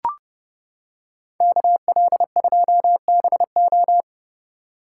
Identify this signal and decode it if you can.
Morse code, heard as a keyed tone, about 30 wpm, 700 Hz. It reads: KL2BO